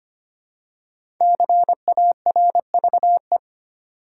Morse code, heard as a keyed tone, about 25 wpm, 700 Hz. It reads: CARVE
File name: 5 Letter Words - First 3 Letters Given - 25wpm